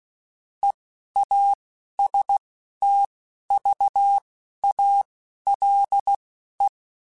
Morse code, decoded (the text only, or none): EASTVALE